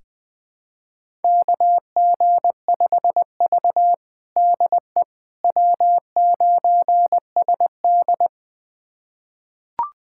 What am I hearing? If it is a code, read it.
Morse code, heard as a keyed tone, about 20 wpm, 700 Hz. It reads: KG5V DE W9SD
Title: USA Callsign DE Callsign - 4 Characters - 20wpm